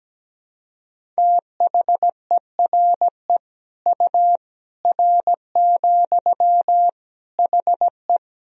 Morse code, heard as a keyed tone, about 17 wpm, 700 Hz. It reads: THERE U R, HE
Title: Sentences from Top 100 Words and Abbreviations 17wpm